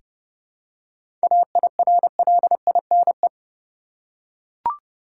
Morse code, heard as a keyed tone, about 30 wpm, 700 Hz. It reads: AIRLINE
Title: Seven Letter Words - 30wpm